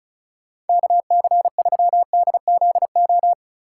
Morse code, heard as a keyed tone, about 36 wpm, 700 Hz. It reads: KC3DZO
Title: USA Call Signs - 6 Characters - 2x Character Spacing - 35wpm